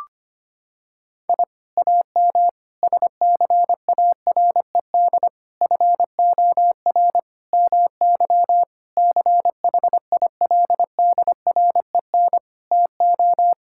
Morse code, heard as a keyed tone, about 25 wpm, 700 Hz. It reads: I AM SCARED FOR MY CHILDREN TO
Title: Sentences from Top 1000 Words - Review Part 5 - 25wpm